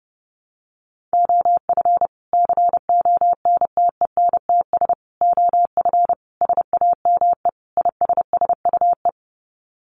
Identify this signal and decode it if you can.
Morse code, heard as a keyed tone, about 30 wpm, 700 Hz. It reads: OF CONTENTS OF SAME ISSUE
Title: Sets of 5 Words Encore2 30wpm